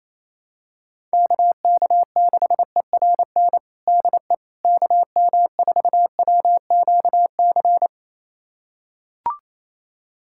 Morse code, heard as a keyed tone, about 28 wpm, 700 Hz. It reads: KK6ERN DE KM4WQC
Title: USA Callsign DE Callsign - 6 Characters - 28wpm